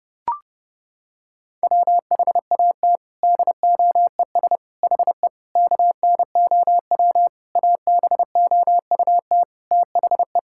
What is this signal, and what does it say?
Morse code, heard as a keyed tone, about 30 wpm, 700 Hz. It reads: WHAT DOES HE KNOW ABOUT THE
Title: Sentences from Top 1600 Words 30wpm